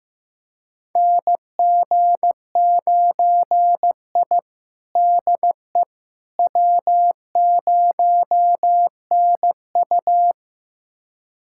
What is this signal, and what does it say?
Morse code, heard as a keyed tone, about 16 wpm, 700 Hz. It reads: NG9I DE W0NU
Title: USA Callsign DE Callsign - 4 Characters - 15wpm